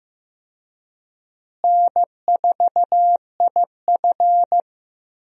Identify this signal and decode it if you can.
Morse code, heard as a keyed tone, about 15 wpm, 700 Hz. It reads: N4IF
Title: USA Call Signs - 4 Characters - Repeated 2x - 15wpm